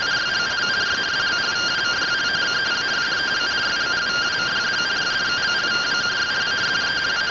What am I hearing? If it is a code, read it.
Two-tone teleprinter signal, about 170 Hz shift.